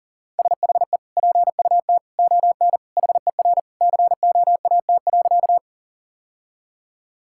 Morse code, heard as a keyed tone, about 40 wpm, 700 Hz. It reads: SHE PUT ON HER COAT.
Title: Sentences from Top 1600 Words 40wpm